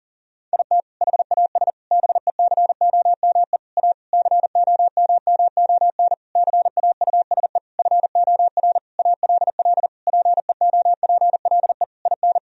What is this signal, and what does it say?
Morse code, heard as a keyed tone, about 40 wpm, 700 Hz. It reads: IT HAS BECOME A COMMON CAUSE FOR ALL PEOPLE IN
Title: Sentences from Top 1000 Words - Review Part 3 - 40wpm